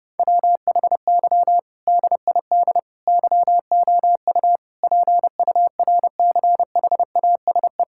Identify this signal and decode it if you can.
Morse code, heard as a keyed tone, about 32 wpm, 700 Hz. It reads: WHY DID YOU PURCHASE
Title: Sentences from Top 1000 Words - Review Part 6 - 30wpm